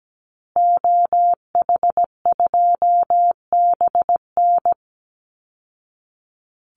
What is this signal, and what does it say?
Morse code, heard as a keyed tone, about 17 wpm, 700 Hz. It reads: OH2BN
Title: Call Signs Repeated 2x - 17wpm